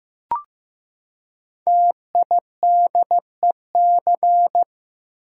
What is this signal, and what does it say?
Morse code, heard as a keyed tone, about 15 wpm, 700 Hz. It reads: TIDEC
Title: Code Groups - 5 Letters - 15wpm